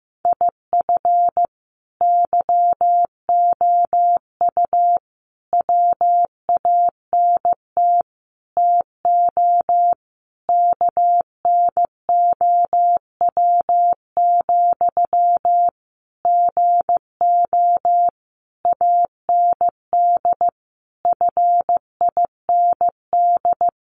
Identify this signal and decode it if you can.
Morse code, heard as a keyed tone, about 15 wpm, 700 Hz. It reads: IF YOU WANT TO KNOW, GO AND FIND
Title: Sentences from Top 200 Words 15wpm